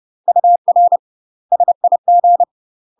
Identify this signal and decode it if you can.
Morse code, heard as a keyed tone, about 30 wpm, 700 Hz. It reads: UR SIG
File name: UK MASTER QSOs TRAINING / 30WPM / UK MASTER QSOs 0027